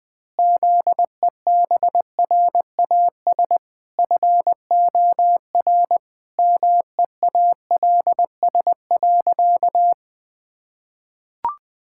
Morse code, heard as a keyed tone, about 20 wpm, 700 Hz. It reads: ZEBRAS FOR MEALS.
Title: Pangram Challenge v2 - 20wpm